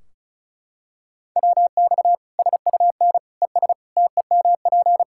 Morse code, heard as a keyed tone, about 34 wpm, 700 Hz. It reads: WX SUN ES TEMP
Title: K1USN SST QSO Practice 35wpm